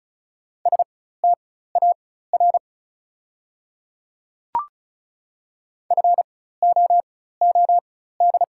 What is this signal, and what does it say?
Morse code, heard as a keyed tone, about 36 wpm, spaced as for 16 wpm, 690 Hz. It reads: STAR FOOD
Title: 4x Character Spacing - Top 500 Words - 35wpm